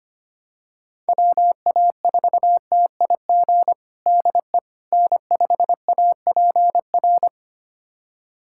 Morse code, heard as a keyed tone, about 25 wpm, 700 Hz. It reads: WA4TIG DE N5APR
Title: USA Callsign DE Callsign 25wpm